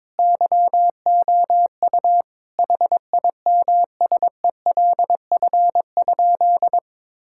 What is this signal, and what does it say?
Morse code, heard as a keyed tone, about 22 wpm, 700 Hz. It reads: YOU HIMSELF?
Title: Sentences from Top 1600 Words 22wpm